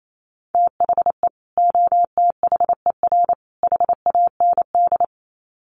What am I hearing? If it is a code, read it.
Morse code, heard as a keyed tone, about 28 wpm, 700 Hz. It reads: THE OTHER HAND
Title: Sets of 3 Words 28wpm